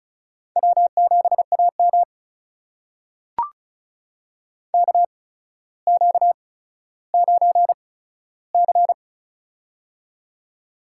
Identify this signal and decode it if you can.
Morse code, heard as a keyed tone, about 36 wpm, spaced as for 15 wpm, 700 Hz. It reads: W7AM KQ9C
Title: USA Call Signs - 4 Characters - 8x Character Spacing - 35wpm